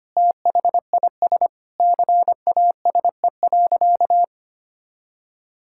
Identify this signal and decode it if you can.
Morse code, heard as a keyed tone, about 25 wpm, 700 Hz. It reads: THIS CASE.
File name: practice / 25wpm / All-Age Appropriate Jokes - 25wpm